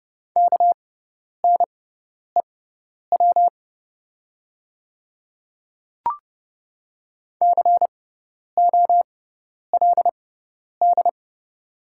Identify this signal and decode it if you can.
Morse code, heard as a keyed tone, about 30 wpm, spaced as for 10 wpm, 700 Hz. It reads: KNEW COLD
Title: 6x Character Spacing - Top 500 Words - 30wpm